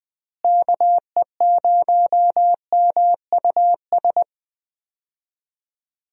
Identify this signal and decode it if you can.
Morse code, heard as a keyed tone, about 20 wpm, 700 Hz. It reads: KE0MUS
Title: USA Call Signs - 6 Characters - Repeated 3x - 20wpm